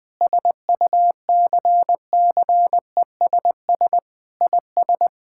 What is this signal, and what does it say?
Morse code, heard as a keyed tone, about 21 wpm, 700 Hz. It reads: SUCCESS IS
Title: Sentences from Top 1000 Words - Review Part 6 - 20wpm